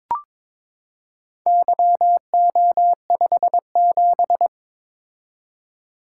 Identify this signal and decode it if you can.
Morse code, heard as a keyed tone, about 22 wpm, 700 Hz. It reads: YO57
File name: Four Letter-Number Combinations - 22wpm